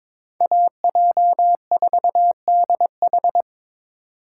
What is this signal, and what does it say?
Morse code, heard as a keyed tone, about 26 wpm, 700 Hz. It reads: AJ4DH
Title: Speed Racing - USA 2x2 Call Signs 22wpm